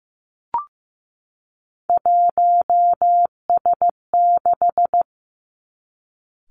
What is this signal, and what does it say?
Morse code, heard as a keyed tone, about 15 wpm, 700 Hz. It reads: 1S6
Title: Three Challenging Letter-Number Combinations 15wpm